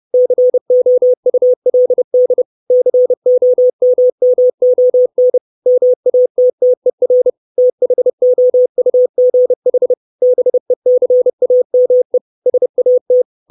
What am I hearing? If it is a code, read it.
Morse code, heard as a keyed tone, about 30 wpm, 500 Hz. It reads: COULD COMMON MATTER THOUGH BECAME SAT
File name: 632 Words 30WPM